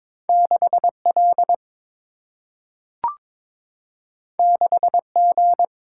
Morse code, heard as a keyed tone, about 22 wpm, 700 Hz. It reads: 6 L 6 G
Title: Single-Letter Single-Number 22wpm